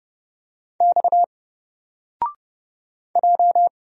Morse code, heard as a keyed tone, about 30 wpm, 700 Hz. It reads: X J
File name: Single-Letter-Number-30wpm